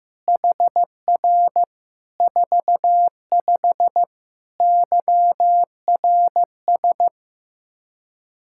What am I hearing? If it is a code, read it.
Morse code, heard as a keyed tone, about 15 wpm, 700 Hz. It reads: HR 45 YRS
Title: Practice QSO - Exchange 3 - More Info - 15wpm